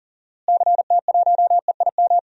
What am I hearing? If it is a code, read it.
Morse code, heard as a keyed tone, about 40 wpm, 700 Hz. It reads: CT1EIM
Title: DXCC-100 Call Signs - 6 Characters - Repeated 3x - 40wpm